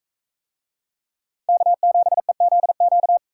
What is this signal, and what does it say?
Morse code, heard as a keyed tone, about 42 wpm, 700 Hz. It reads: K7EZQ